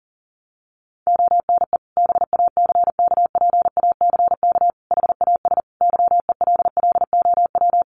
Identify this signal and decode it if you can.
Morse code, heard as a keyed tone, about 40 wpm, 700 Hz. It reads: ONE BACKPACK HAS YELLOW
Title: Additional Easy Phrases 40wpm